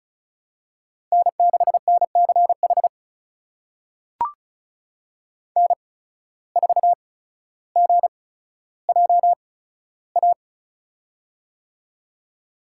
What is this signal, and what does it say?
Morse code, heard as a keyed tone, about 36 wpm, spaced as for 15 wpm, 700 Hz. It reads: N6NCH N4GJA
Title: USA Call Signs - 5 Characters - 8x Character Spacing - 35wpm